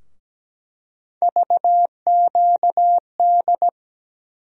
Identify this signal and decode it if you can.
Morse code, heard as a keyed tone, about 17 wpm, 700 Hz. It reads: VQD
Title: USA Call Sign Postfixes - 3 Letters 17wpm